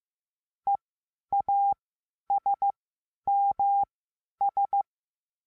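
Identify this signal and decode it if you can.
Morse code, heard as a keyed tone, about 15 wpm, spaced as for 10 wpm, 800 Hz. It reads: EASMS
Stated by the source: Morse lesson adding the letter A, sending random five-letter practice groups.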